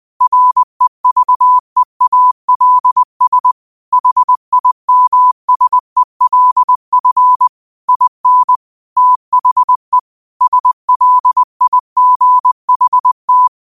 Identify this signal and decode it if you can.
Morse code, heard as a keyed tone, about 20 wpm, 1000 Hz. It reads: REVEALS HIMSELF IN THE SLIGHT
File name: qotdfiles / 20wpm